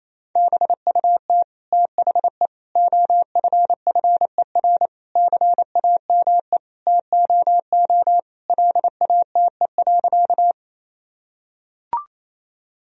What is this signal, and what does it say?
Morse code, heard as a keyed tone, about 28 wpm, 700 Hz. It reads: BUT THE OFFER CAME TOO LATE.
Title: Sentences from Top 1000 Words - Review Part 4 - 28wpm